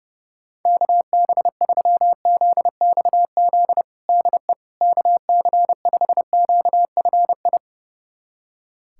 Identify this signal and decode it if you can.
Morse code, heard as a keyed tone, about 31 wpm, 700 Hz. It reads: KB3ZXZ DE KC5QFI
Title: USA Callsign DE Callsign - 6 Characters - 30wpm